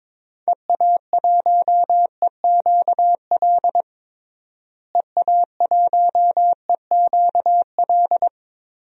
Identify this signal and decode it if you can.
Morse code, heard as a keyed tone, about 21 wpm, 700 Hz. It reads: EA1EQL EA1EQL
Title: DXCC-100 Call Signs - 6 Characters - Repeated 2x - 22wpm